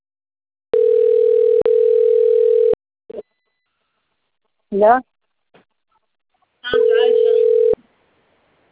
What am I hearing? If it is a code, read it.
Ringback tone (North America), 440 + 480 Hz, with speech.
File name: autotestagent / 10-53-47.136..callcentertransfer. .a8d151c1-5ace-42eb-8e80